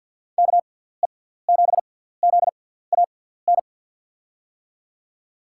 Morse code, heard as a keyed tone, about 53 wpm, spaced as for 17 wpm, 700 Hz. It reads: KE7ZAN
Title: USA Call Signs - 6x Character Spacing - 50wpm